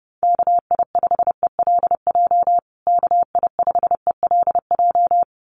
Morse code, heard as a keyed tone, about 30 wpm, 700 Hz. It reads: KI5ELJ KI5ELJ
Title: USA Call Signs Repeated 2x 30wpm